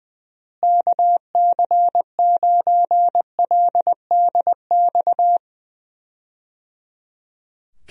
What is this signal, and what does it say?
Morse code, heard as a keyed tone, about 20 wpm, 700 Hz. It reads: KC9LDX
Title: USA Call Signs - 6 Characters - Repeated 3x - 20wpm